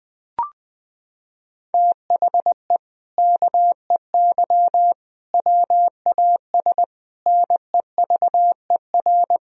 Morse code, heard as a keyed tone, about 20 wpm, 700 Hz. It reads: THE KEY WAS NEVER
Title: Sentences from Top 1000 Words - Review Part 4 - 20wpm